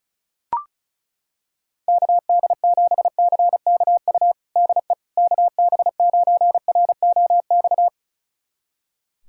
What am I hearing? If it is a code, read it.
Morse code, heard as a keyed tone, about 36 wpm, 700 Hz. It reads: KD7CKU DE KB9ROX
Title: USA Callsign DE Callsign 35wpm